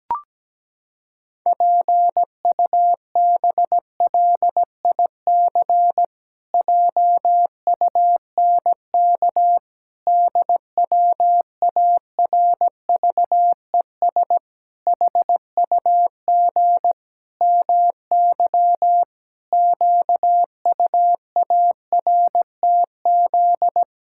Morse code, heard as a keyed tone, about 17 wpm, 700 Hz. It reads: PUBLIC JUNK DWARVES HUG MY QUARTZ
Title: Pangram Challenge v2 - 17wpm